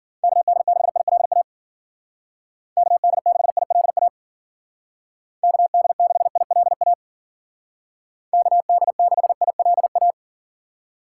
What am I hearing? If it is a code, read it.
Morse code, heard as a keyed tone, about 49 wpm, 700 Hz. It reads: KD6ILA KD6ILA KD6ILA KD6ILA
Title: Speed Racing - USA 2x3 Call Signs 40wpm